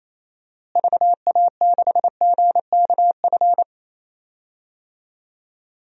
Morse code, heard as a keyed tone, about 28 wpm, 700 Hz. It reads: VA6GKF
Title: DXCC-100 Call Signs - 6 Characters - Repeated 3x - 28wpm